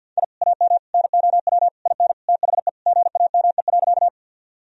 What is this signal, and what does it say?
Morse code, heard as a keyed tone, about 50 wpm, 700 Hz. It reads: I AM NOW IN THE GAME.
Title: Sentences from Top 1000 Words - Review Part 3 - 50wpm